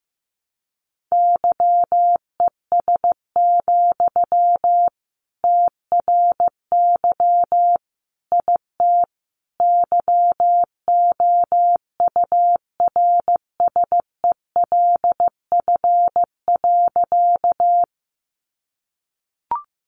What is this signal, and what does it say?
Morse code, heard as a keyed tone, about 15 wpm, 700 Hz. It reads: YES, TRY IT YOURSELF.
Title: Sentences from Top 600 Words 15wpm